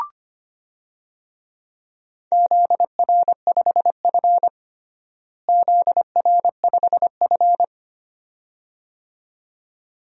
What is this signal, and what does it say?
Morse code, heard as a keyed tone, about 25 wpm, 700 Hz. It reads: ZR5F ZR5F
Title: DXCC-100 Call Signs - 4 Characters - Repeated 2x - 25wpm